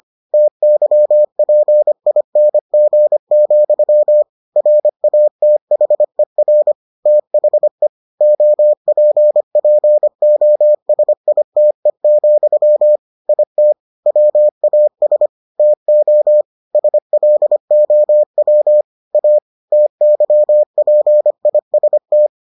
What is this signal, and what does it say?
Morse code, heard as a keyed tone, about 25 wpm, 600 Hz. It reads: TYPING, RATHER THE OPPOSITE, IT WAS TO SLOW A TYPIST